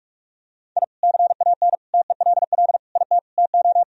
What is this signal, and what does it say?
Morse code, heard as a keyed tone, about 46 wpm, 700 Hz. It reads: I CAN TELL IT TO
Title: Sentences from Top 1000 Words - Review Part 1 - 45wpm